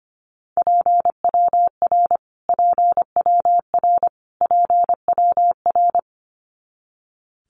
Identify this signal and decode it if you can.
Morse code, heard as a keyed tone, about 25 wpm, 700 Hz. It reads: PWR PWR PWR
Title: Abbreviations Repeated 3x - 25wpm